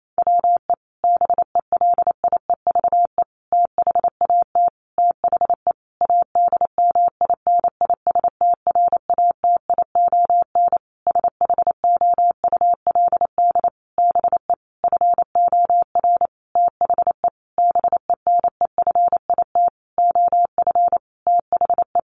Morse code, heard as a keyed tone, about 28 wpm, 700 Hz. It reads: WE BELIEVE THAT THE ADMINISTRATION SHOULD BE FOR THE BENEFIT OF THE
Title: Sentences from Top 1300 Words 28wpm